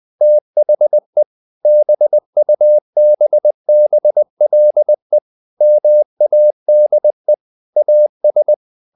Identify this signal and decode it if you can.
Morse code, heard as a keyed tone, about 20 wpm, 600 Hz. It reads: THE BUBBLE MADE AS
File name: the old man and the sea / the old man and the sea part 1 20wpm